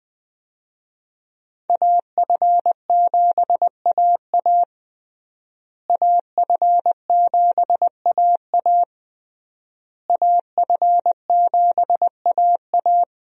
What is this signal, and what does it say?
Morse code, heard as a keyed tone, about 20 wpm, 700 Hz. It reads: AF7AA AF7AA AF7AA
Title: USA Call Signs - 5 Characters - Repeated 3x - 20wpm